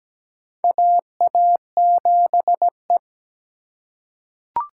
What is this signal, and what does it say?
Morse code, heard as a keyed tone, about 17 wpm, 700 Hz. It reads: AA7E
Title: USA Call Signs - 3x Character Spacing - 17wpm